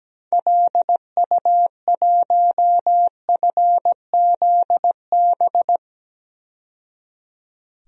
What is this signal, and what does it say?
Morse code, heard as a keyed tone, about 17 wpm, 700 Hz. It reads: LU1FZB
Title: DXCC-100 Call Signs - 6 Characters - 17wpm